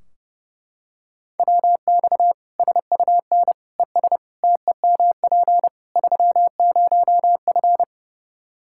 Morse code, heard as a keyed tone, about 29 wpm, 700 Hz. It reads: WX SUN ES TEMP 30F
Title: Practice QSO - Exchange 2 - Info - 30wpm